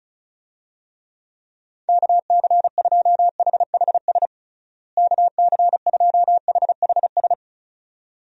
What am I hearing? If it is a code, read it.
Morse code, heard as a keyed tone, about 36 wpm, 700 Hz. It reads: KC2HHS KC2HHS
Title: USA Call Signs - 6 Characters - Repeated 2x - 35wpm